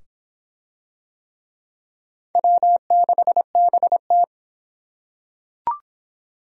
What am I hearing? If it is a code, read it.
Morse code, heard as a keyed tone, about 23 wpm, 700 Hz. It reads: W6BT